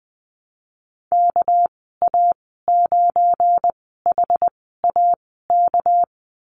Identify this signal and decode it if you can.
Morse code, heard as a keyed tone, about 20 wpm, spaced as for 15 wpm, 700 Hz. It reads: KA9HAK
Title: 238 Lesson 38g - Letter q - Call Signs 20wpm